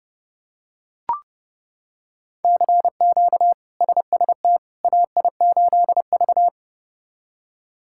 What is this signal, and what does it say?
Morse code, heard as a keyed tone, about 32 wpm, 700 Hz. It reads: CQ SST AI8V
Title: K1USN SST QSO Practice - 30wpm